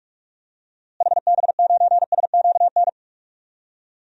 Morse code, heard as a keyed tone, about 46 wpm, 700 Hz. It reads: HB9SQN
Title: DXCC-100 Call Signs - 6 Characters - Repeated 2x - 45wpm